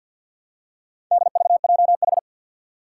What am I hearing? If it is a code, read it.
Morse code, heard as a keyed tone, about 53 wpm, 700 Hz. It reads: DVJH
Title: Four Challenging Letter-Number Combinations 50wpm